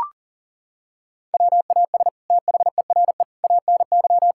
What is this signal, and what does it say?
Morse code, heard as a keyed tone, about 40 wpm, 700 Hz. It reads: WAS THERE ANY
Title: Sentences from Top 1000 Words - Review Part 6 - 40wpm